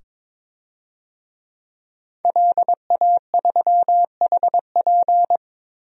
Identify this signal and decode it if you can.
Morse code, heard as a keyed tone, about 21 wpm, 700 Hz. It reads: LA3HP
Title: DXCC-100 Call Signs - 5 Characters - 22wpm